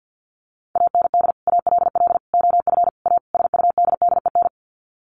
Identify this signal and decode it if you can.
Morse code, heard as a keyed tone, about 50 wpm, 700 Hz. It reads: AND ALL OF A SUDDEN
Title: Sets of 5 Words 50wpm